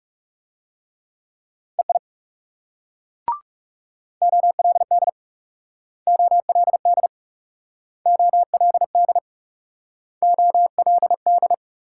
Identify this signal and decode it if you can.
Morse code, heard as a keyed tone, about 36 wpm, 700 Hz. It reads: EI OLD OLD OLD OLD
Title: Speed-Racing Multi-letter Phonemes - 30wpm